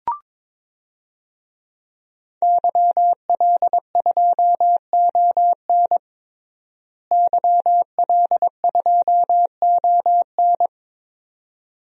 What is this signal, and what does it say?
Morse code, heard as a keyed tone, about 21 wpm, 700 Hz. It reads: YL2ON YL2ON
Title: DXCC-100 Call Signs - 5 Characters - Repeated 2x - 22wpm